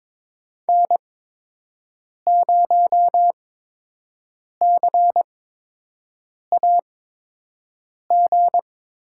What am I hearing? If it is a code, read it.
Morse code, heard as a keyed tone, about 22 wpm, spaced as for 6 wpm, 700 Hz. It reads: N0CAG
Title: USA Call Signs - 5 Characters - 8x Character Spacing - 22wpm